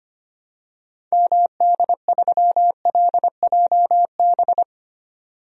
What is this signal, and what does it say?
Morse code, heard as a keyed tone, about 25 wpm, 700 Hz. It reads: MD3LJB
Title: DXCC-100 Call Signs - 6 Characters - 25wpm